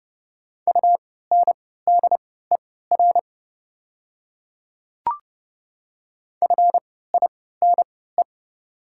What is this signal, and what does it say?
Morse code, heard as a keyed tone, about 30 wpm, spaced as for 17 wpm, 690 Hz. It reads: UNDER FINE
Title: 3x Character Spacing - Top 500 Words - 30wpm